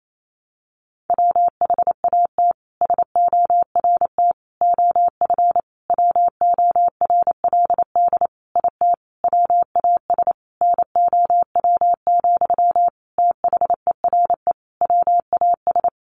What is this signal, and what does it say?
Morse code, heard as a keyed tone, about 28 wpm, 700 Hz. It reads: WHAT SORT OF WORLD IT WAS NOW, THERE WAS
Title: A Sound Of Thunder 28wpm